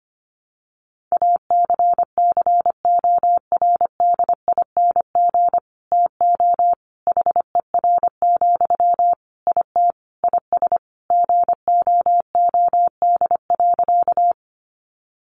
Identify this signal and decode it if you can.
Morse code, heard as a keyed tone, about 25 wpm, 700 Hz. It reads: ACCORDING TO HER, IT IS GOOD.
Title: Sentences from Top 800 Words 25wpm